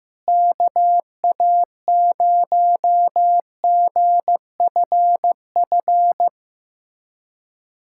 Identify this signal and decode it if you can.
Morse code, heard as a keyed tone, about 15 wpm, 700 Hz. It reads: KA0GFF
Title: USA Callsign DE Callsign 15wpm